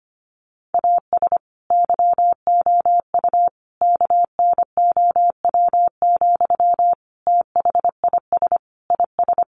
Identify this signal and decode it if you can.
Morse code, heard as a keyed tone, about 25 wpm, 700 Hz. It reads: AS YOU KNOW, THIS IS
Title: Sentences from Top 1100 Words 25wpm